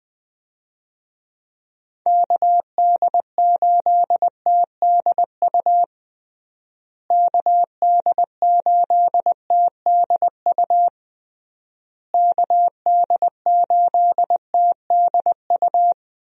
Morse code, heard as a keyed tone, about 20 wpm, 700 Hz. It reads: KD8TDU KD8TDU KD8TDU
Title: USA Call Signs - 6 Characters - Repeated 3x - 20wpm